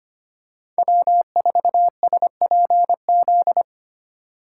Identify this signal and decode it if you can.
Morse code, heard as a keyed tone, about 25 wpm, 700 Hz. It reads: W4SPZ
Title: USA Call Signs - 3x Character Spacing - 25wpm